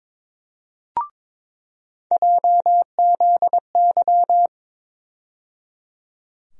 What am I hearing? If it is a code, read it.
Morse code, heard as a keyed tone, about 22 wpm, 700 Hz. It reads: JZY
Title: USA Call Sign Postfixes - 3 Letters 22wpm